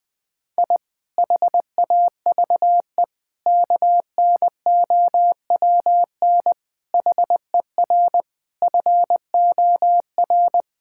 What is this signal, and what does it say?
Morse code, heard as a keyed tone, about 20 wpm, 700 Hz. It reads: I HAVE KNOWN HER FOR
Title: Sentences with 5-Word Phrases - Spoken Beginning - 20wpm